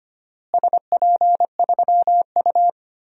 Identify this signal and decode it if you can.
Morse code, heard as a keyed tone, about 25 wpm, 700 Hz. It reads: SP3U
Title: DXCC-100 Call Signs - 4 Characters - Repeated 2x - 25wpm